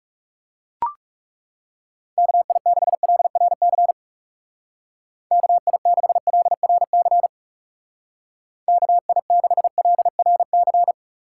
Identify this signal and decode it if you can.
Morse code, heard as a keyed tone, about 36 wpm, 700 Hz. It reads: KI6LRC KI6LRC KI6LRC
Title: Speed Racing - USA Call Signs 30wpm